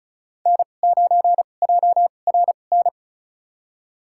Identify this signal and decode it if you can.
Morse code, heard as a keyed tone, about 36 wpm, spaced as for 25 wpm, 700 Hz. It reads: N9JRN
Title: USA Call Signs - 5 Characters - 2x Character Spacing - 35wpm